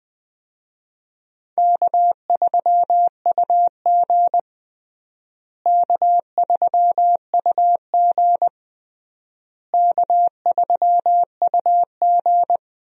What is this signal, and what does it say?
Morse code, heard as a keyed tone, about 21 wpm, 700 Hz. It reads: K3UG K3UG K3UG
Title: USA Call Signs - 4 Characters - Repeated 3x - 20wpm